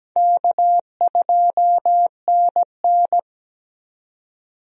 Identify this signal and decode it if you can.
Morse code, heard as a keyed tone, about 17 wpm, 700 Hz. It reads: K2NN
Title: USA Call Signs - 4 Characters - 17wpm